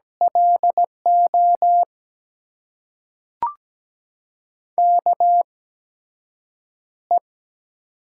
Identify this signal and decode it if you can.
Morse code, heard as a keyed tone, about 17 wpm, spaced as for 7 wpm, 700 Hz. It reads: LO KE
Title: USA Call Signs - 6 Characters - 8x Character Spacing - 17wpm